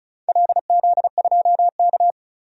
Morse code, heard as a keyed tone, about 36 wpm, 700 Hz. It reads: LZ2K
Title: DXCC-100 Call Signs - 4 Characters - 35wpm